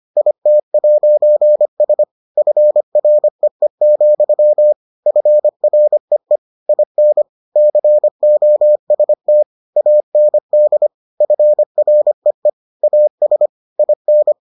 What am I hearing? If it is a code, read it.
Morse code, heard as a keyed tone, about 25 wpm, 600 Hz. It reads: IT'S FREE, FREE IN COST AND FREE AS IN